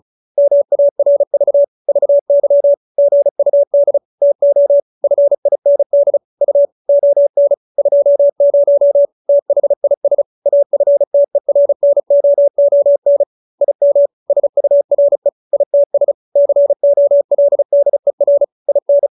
Morse code, heard as a keyed tone, about 35 wpm, 570 Hz. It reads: MARV VY GUD TO FIND U ON 20 THIS AFTERNOON IM SURE ITS COLDER IN
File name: QSO312 35